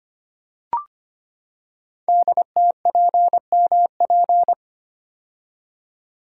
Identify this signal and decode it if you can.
Morse code, heard as a keyed tone, about 25 wpm, 700 Hz. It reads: DTPMP
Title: Code Groups - 5 Letters - 25wpm